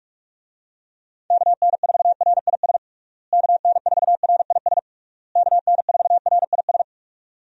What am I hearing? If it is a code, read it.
Morse code, heard as a keyed tone, about 41 wpm, 700 Hz. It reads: KN4RIS KN4RIS KN4RIS
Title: USA Call Signs - 6 Characters - Repeated 3x - 45wpm